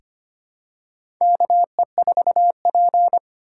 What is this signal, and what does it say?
Morse code, heard as a keyed tone, about 25 wpm, 700 Hz. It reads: KE4P
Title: USA Call Signs - 4 Characters - 4x Character Spacing - 25wpm